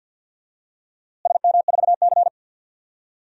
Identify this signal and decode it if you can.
Morse code, heard as a keyed tone, about 49 wpm, 700 Hz. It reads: SM4C
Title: DXCC-100 Call Signs - 4 Characters - Repeated 2x - 50wpm